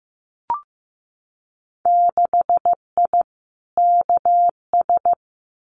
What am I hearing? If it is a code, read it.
Morse code, heard as a keyed tone, about 15 wpm, 700 Hz. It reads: 6I KS
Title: Winter Field Day Exchanges 15wpm